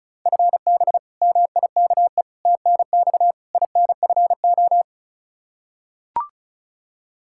Morse code, heard as a keyed tone, about 36 wpm, 700 Hz. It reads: FB MIKE TNX INFO
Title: Practice QSO - Exchange 3 - More Info - 35wpm